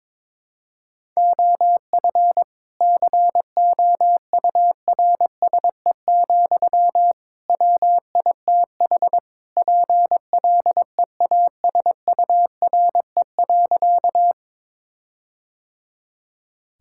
Morse code, heard as a keyed tone, about 22 wpm, 700 Hz. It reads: OF COURSE, WITH PLEASURE.
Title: Sentences from Top 1600 Words 22wpm